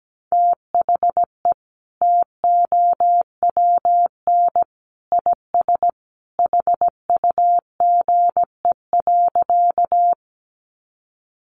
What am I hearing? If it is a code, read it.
Morse code, heard as a keyed tone, about 17 wpm, 700 Hz. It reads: THE TOWN IS HUGE.
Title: Sentences from Top 1100 Words 17wpm